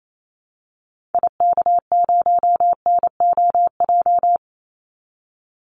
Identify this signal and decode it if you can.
Morse code, heard as a keyed tone, about 28 wpm, 700 Hz. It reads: IK0NOJ
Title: Call Signs Repeated 2x - 28wpm